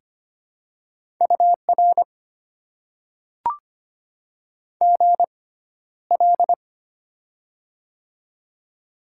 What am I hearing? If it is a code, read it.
Morse code, heard as a keyed tone, about 25 wpm, spaced as for 13 wpm, 690 Hz. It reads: UR GL
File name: Top 90 QSO Elements - 6x Character Spacing - 25wpm